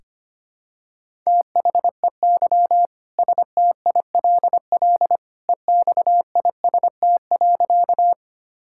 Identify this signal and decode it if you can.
Morse code, heard as a keyed tone, about 25 wpm, 700 Hz. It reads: THEY STILL EXIST.
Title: Sentences from Top 1000 Words - Review Part 5 - 25wpm